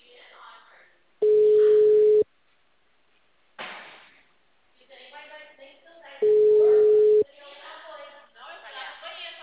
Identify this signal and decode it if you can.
Ringback tone (Europe, 425 Hz), with speech.